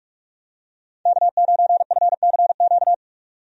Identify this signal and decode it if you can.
Morse code, heard as a keyed tone, about 46 wpm, 700 Hz. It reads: K9FCQ
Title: USA Call Signs - 5 Characters - 45wpm